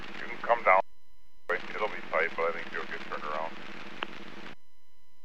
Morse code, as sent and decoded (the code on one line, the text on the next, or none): none